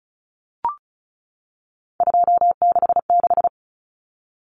Morse code, heard as a keyed tone, about 34 wpm, 700 Hz. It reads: 266
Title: Common Cut Numbers - 3-Digit 35wpm